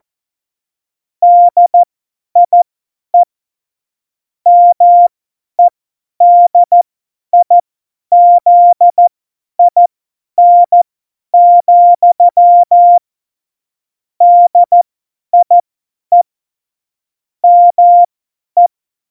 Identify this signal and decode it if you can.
Morse code, heard as a keyed tone, about 14 wpm, spaced as for 10 wpm, 700 Hz. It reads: DIE MEDIZIN, DIE ME
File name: Morsekurs / Witze / Witz 11. Februar 2020, 50 F70